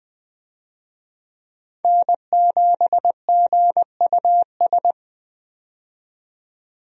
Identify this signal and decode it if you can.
Morse code, heard as a keyed tone, about 20 wpm, 700 Hz. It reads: N7GUS
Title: USA Call Signs - 5 Characters - 20wpm